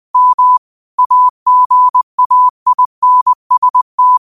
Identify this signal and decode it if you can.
Morse code, heard as a keyed tone, about 20 wpm, 1000 Hz. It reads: M AGAINST
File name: qotdfiles / 20wpm